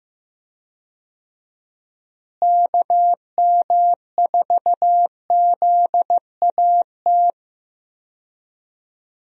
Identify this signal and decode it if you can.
Morse code, heard as a keyed tone, about 15 wpm, 700 Hz. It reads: KM4ZAT
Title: USA Call Signs - 6 Characters - Repeated 3x - 15wpm